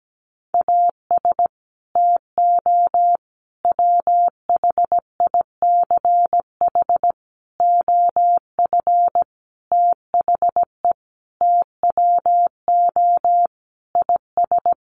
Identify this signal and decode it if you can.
Morse code, heard as a keyed tone, about 17 wpm, 700 Hz. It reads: AS TO WHICH OF THE TWO IS
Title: Sentences from Top 1200 Words 17wpm